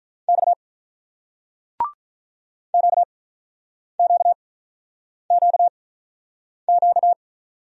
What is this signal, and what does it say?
Morse code, heard as a keyed tone, about 47 wpm, spaced as for 9 wpm, 700 Hz. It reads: X QQQQ